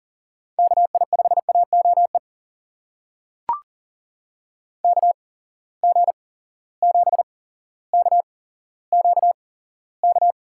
Morse code, heard as a keyed tone, about 40 wpm, spaced as for 17 wpm, 700 Hz. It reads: KI5AOE KG7KQK
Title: USA Call Signs - 8x Character Spacing - 40wpm